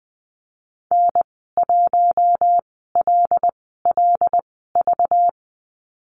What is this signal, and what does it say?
Morse code, heard as a keyed tone, about 19 wpm, spaced as for 14 wpm, 700 Hz. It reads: N1LLV
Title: 193 Lesson 31g - Letter v - Call Signs 20wpm